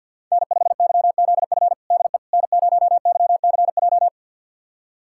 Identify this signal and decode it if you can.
Morse code, heard as a keyed tone, about 51 wpm, 700 Hz. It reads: N5YZF DE N0YCJ
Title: USA Callsign DE Callsign - 5 Characters - 50wpm